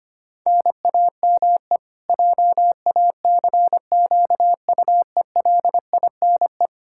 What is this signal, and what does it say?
Morse code, heard as a keyed tone, about 25 wpm, 700 Hz. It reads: NAME JACQUELINE
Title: Practice QSO - Exchange 1 - Beginning - 25wpm